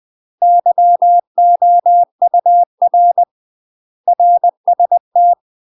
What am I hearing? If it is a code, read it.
Morse code, heard as a keyed tone, about 20 wpm, 700 Hz. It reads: YOUR RST